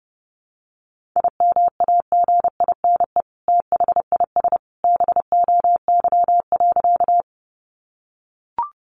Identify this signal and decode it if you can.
Morse code, heard as a keyed tone, about 30 wpm, 700 Hz. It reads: IMAGINE THIS BOY.
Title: Sentences from Top 1400 Words 30wpm